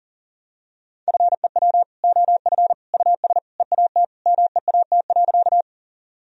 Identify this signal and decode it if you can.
Morse code, heard as a keyed tone, about 40 wpm, 700 Hz. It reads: FEW OF US EAT MEAT.
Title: Sentences from Top 1000 Words - Review Part 3 - 40wpm